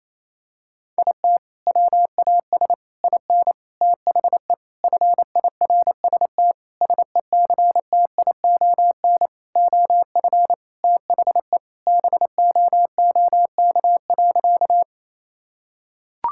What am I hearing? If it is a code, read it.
Morse code, heard as a keyed tone, about 28 wpm, 700 Hz. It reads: IT WAS IN THE FIRST SECTION OF THE BOOK.
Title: Sentences from Top 1000 Words - Review Part 5 - 28wpm